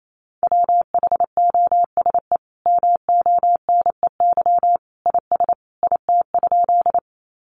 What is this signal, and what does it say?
Morse code, heard as a keyed tone, about 28 wpm, 700 Hz. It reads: WHOSE MONEY IS IT?
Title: Sentences from Top 1200 Words 28wpm